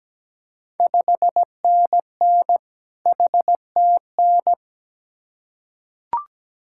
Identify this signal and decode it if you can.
Morse code, heard as a keyed tone, about 17 wpm, 700 Hz. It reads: 5NN HTN
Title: Ontario QSO Party Exchange 17wpm